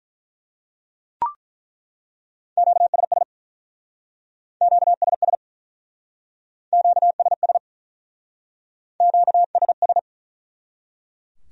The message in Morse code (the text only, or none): QSS QSS QSS QSS